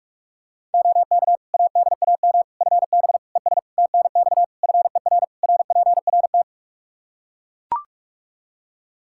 Morse code, heard as a keyed tone, about 41 wpm, 700 Hz. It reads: OK ADAM FB ES TNX FER RPRT
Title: Practice QSO - Exchange 2 - Info - 45wpm